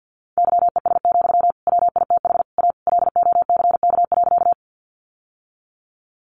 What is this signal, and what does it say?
Morse code, heard as a keyed tone, about 50 wpm, 700 Hz. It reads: YES, WITH A LOCK.
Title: Sentences from Top 1300 Words 50wpm